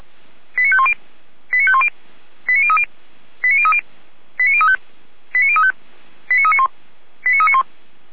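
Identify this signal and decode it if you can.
Five-tone selective call (ZVEI), heard as quick runs of five tones, tones of about 70 ms. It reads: 89510 89510 89030 89030 89035 89035 89391 89391
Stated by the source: ZVEI five-tone selective call received on an SDR.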